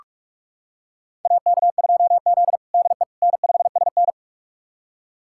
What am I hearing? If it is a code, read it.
Morse code, heard as a keyed tone, about 46 wpm, 700 Hz. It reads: AK2Z DE N5SN
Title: USA Callsign DE Callsign - 4 Characters - 45wpm